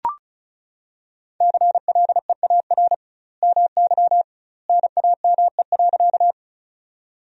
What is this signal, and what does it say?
Morse code, heard as a keyed tone, about 36 wpm, 700 Hz. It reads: CLEAR MY NAME.
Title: 2x Word-Spacing - Sentences from Top 500 Words - 35wpm